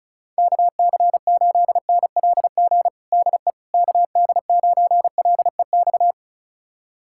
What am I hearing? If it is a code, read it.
Morse code, heard as a keyed tone, about 36 wpm, 700 Hz. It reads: KC8NLG DE KD9LEX
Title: USA Callsign DE Callsign - 6 Characters - 35wpm